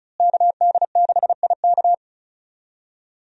Morse code, heard as a keyed tone, about 36 wpm, 700 Hz. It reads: KD6IK
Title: USA Call Signs - 5 Characters - Repeated 2x - 35wpm